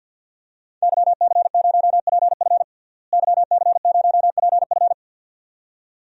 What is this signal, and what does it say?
Morse code, heard as a keyed tone, about 52 wpm, 700 Hz. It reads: YC0PF YC0PF
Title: DXCC-100 Call Signs - 5 Characters - Repeated 2x - 50wpm